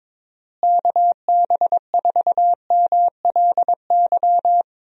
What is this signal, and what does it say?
Morse code, heard as a keyed tone, about 22 wpm, 700 Hz. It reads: KB4MLY
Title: USA Call Signs - 6 Characters - Repeated 2x - 22wpm